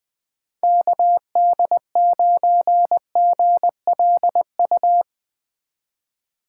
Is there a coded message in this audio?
Yes: Morse code from 0.6 to 5.0 s, heard as a keyed tone, about 20 wpm, 700 Hz. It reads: KD9GLU